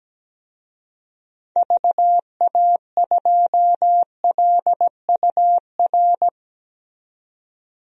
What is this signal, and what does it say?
Morse code, heard as a keyed tone, about 17 wpm, 700 Hz. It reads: VA2LUR
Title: DXCC-100 Call Signs - 6 Characters - 17wpm